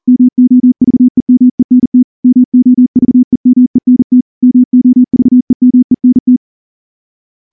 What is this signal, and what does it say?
Morse code, heard as a keyed tone, about 38 wpm, 250 Hz. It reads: MOVEMENT MOVEMENT MOVEMENT
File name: morse / morse-training-40WPM